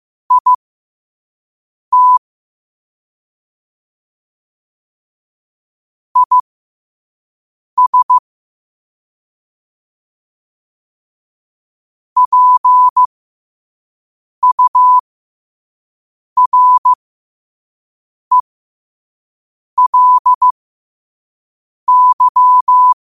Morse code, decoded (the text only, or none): IT IS PURELY